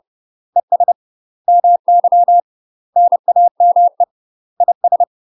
Morse code, heard as a keyed tone, about 30 wpm, 700 Hz. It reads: ES MY NAME IS
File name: UK MASTER QSOs TRAINING / 30WPM / UK MASTER QSOs 0034